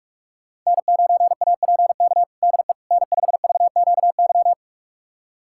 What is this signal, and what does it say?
Morse code, heard as a keyed tone, about 41 wpm, 700 Hz. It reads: N9APK DE N5VQY
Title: USA Callsign DE Callsign - 5 Characters - 45wpm